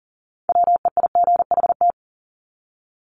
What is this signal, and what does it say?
Morse code, heard as a keyed tone, about 40 wpm, 700 Hz. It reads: WEIGHT
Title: Top 500 Words 40wpm